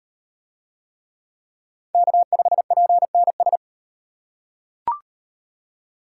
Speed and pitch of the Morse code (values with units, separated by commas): 36 wpm, 700 Hz